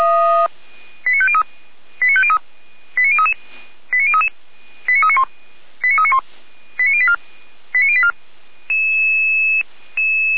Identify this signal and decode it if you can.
Five-tone selective call (ZVEI), heard as quick runs of five tones, tones of about 70 ms. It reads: 89503 89033 89033 89391 89391 89084 89084